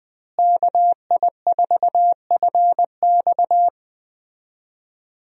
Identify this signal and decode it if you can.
Morse code, heard as a keyed tone, about 20 wpm, 700 Hz. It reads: KI4FX
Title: USA Call Signs - 5 Characters - Repeated 3x - 20wpm